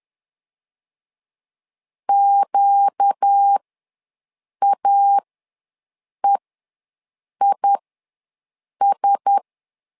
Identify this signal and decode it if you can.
Morse code, heard as a keyed tone, about 10 wpm, spaced as for 6 wpm, 800 Hz. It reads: QAEIS
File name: LEZIONI / 4 Q1S ws 10wpm 2208